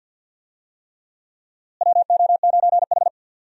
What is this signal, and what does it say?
Morse code, heard as a keyed tone, about 50 wpm, 700 Hz. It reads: WO9H
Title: USA Call Signs - 4 Characters - Repeated 3x - 50wpm